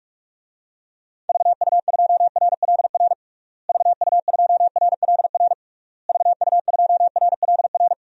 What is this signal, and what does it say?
Morse code, heard as a keyed tone, about 46 wpm, 700 Hz. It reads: VU2RLR VU2RLR VU2RLR
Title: DXCC-100 Call Signs - 6 Characters - Repeated 3x - 45wpm